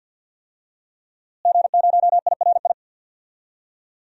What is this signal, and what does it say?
Morse code, heard as a keyed tone, about 52 wpm, 700 Hz. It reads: G0IRI
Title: DXCC-100 Call Signs - 5 Characters - Repeated 3x - 50wpm